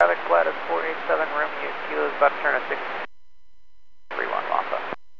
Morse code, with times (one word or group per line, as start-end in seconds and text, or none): none